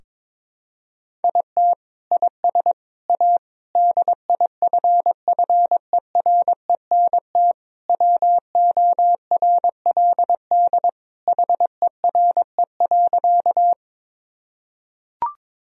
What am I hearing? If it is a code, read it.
Morse code, heard as a keyed tone, about 21 wpm, 700 Hz. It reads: IT IS A DIFFERENT WORLD HERE.
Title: Sentences from Top 1000 Words - Review Part 3 - 22wpm